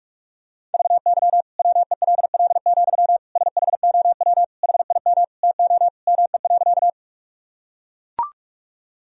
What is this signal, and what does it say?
Morse code, heard as a keyed tone, about 41 wpm, 700 Hz. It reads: VY WELL, SHOW HIM TO ME.
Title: Sentences from Top 200 Words and Abbreviations 45wpm